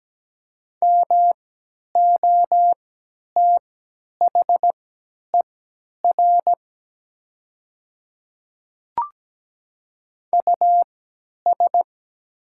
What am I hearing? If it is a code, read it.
Morse code, heard as a keyed tone, about 17 wpm, spaced as for 10 wpm, 700 Hz. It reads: MOTHER US
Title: 3x Character Spacing - Top 200 Words - 17wpm